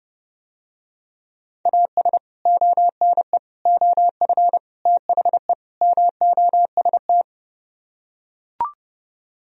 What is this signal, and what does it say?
Morse code, heard as a keyed tone, about 32 wpm, 700 Hz. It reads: AS ONE OF THE MOST